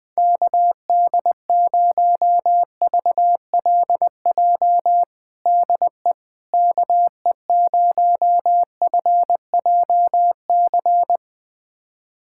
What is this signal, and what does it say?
Morse code, heard as a keyed tone, about 20 wpm, 700 Hz. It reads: KD0VLJ DE KE0FJC
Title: USA Callsign DE Callsign - 6 Characters - 20wpm